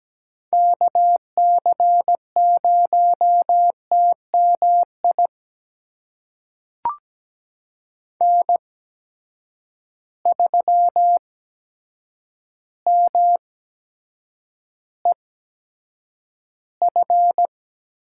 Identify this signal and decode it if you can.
Morse code, heard as a keyed tone, about 17 wpm, spaced as for 8 wpm, 700 Hz. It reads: KC0TMI N3MEF
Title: USA Call Signs - 8x Character Spacing - 17wpm